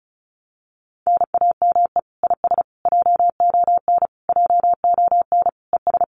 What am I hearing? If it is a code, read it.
Morse code, heard as a keyed tone, about 36 wpm, 700 Hz. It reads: NAME IS JON JON ES
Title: Common QSO Phrases Encore 35wpm